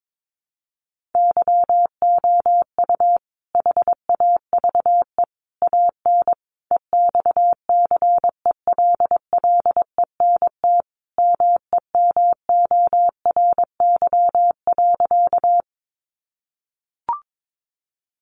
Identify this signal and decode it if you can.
Morse code, heard as a keyed tone, about 22 wpm, 700 Hz. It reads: YOU HAVE AN EXCELLENT MEMORY.
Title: Sentences from Top 1300 Words 22wpm